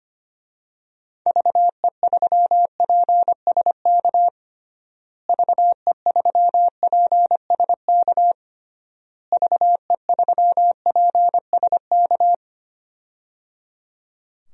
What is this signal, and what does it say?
Morse code, heard as a keyed tone, about 25 wpm, 690 Hz. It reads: VE3PSK VE3PSK VE3PSK
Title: DXCC-100 Call Signs - 6 Characters - Repeated 3x - 25wpm